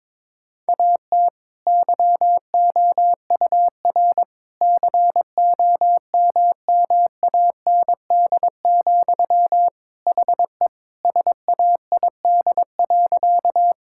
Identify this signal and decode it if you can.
Morse code, heard as a keyed tone, about 22 wpm, 700 Hz. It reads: AT YOUR COMMAND, HE SAID.
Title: Sentences from Top 1600 Words 22wpm